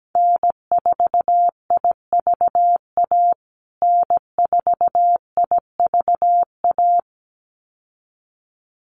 Morse code, heard as a keyed tone, about 17 wpm, 700 Hz. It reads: N4IVA N4IVA
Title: USA Call Signs Repeated 2x 17wpm